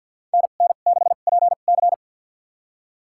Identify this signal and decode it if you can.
Morse code, heard as a keyed tone, about 52 wpm, spaced as for 37 wpm, 700 Hz. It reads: NN6PC
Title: USA Call Signs - 5 Characters - 2x Character Spacing - 50wpm